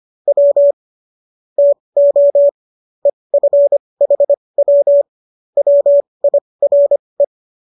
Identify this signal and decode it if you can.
Morse code, heard as a keyed tone, about 25 wpm, spaced as for 20 wpm, 570 Hz. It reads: W TO EFHW WIRE